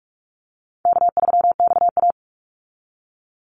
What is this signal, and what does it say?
Morse code, heard as a keyed tone, about 44 wpm, 700 Hz. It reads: K3XA
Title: Easy Call Signs Repeated 2x - 45wpm